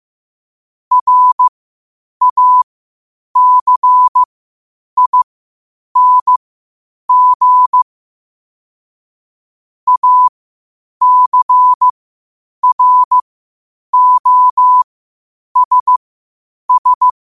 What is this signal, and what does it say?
Morse code, heard as a keyed tone, about 15 wpm, spaced as for 8 wpm, 1000 Hz. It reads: RACING ACROSS
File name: qotdfiles / 8wpm